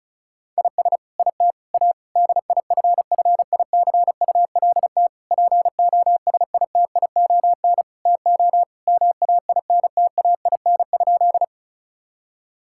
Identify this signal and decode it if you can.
Morse code, heard as a keyed tone, about 34 wpm, 700 Hz. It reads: IS IT A DIFFICULT POSITION TO MAINTAIN?
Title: Sentences from Top 1000 Words - Review Part 5 - 35wpm